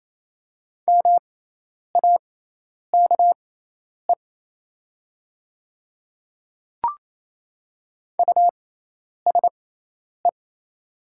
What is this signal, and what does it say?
Morse code, heard as a keyed tone, about 28 wpm, spaced as for 10 wpm, 690 Hz. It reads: MAKE USE
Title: 6x Character Spacing - Top 100 Words - 28wpm